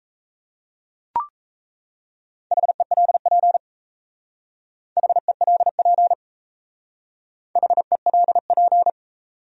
Morse code, heard as a keyed tone, about 35 wpm, 700 Hz. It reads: HELP HELP HELP
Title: Speed-Racing Top 200 Words 28wpm